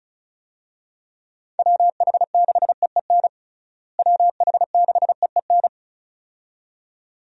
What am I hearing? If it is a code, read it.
Morse code, heard as a keyed tone, about 34 wpm, 700 Hz. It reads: WH6EEN WH6EEN
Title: USA Call Signs - 6 Characters - Repeated 2x - 35wpm